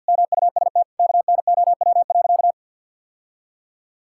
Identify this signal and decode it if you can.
Morse code, heard as a keyed tone, about 50 wpm, 700 Hz. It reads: MUST KNOW.
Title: Sentences from Top 1000 Words - Review Part 6 - 50wpm